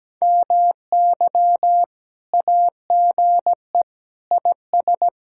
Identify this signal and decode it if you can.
Morse code, heard as a keyed tone, about 17 wpm, 700 Hz. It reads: MY AGE IS
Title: Speed-Racing Common QSO Phrases Encore - 17wpm